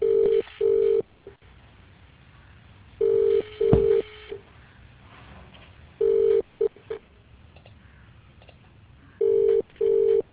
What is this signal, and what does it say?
Ringback tone (UK), 400 + 450 Hz.